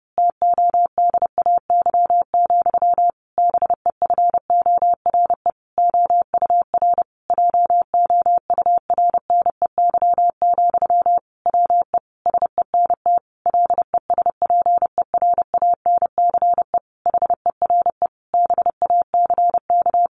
Morse code, heard as a keyed tone, about 28 wpm, 700 Hz. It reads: TODAY, BEFORE OUR JOURNEY, WE SENT LESPERANCE HERE BACK
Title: A Sound Of Thunder 30wpm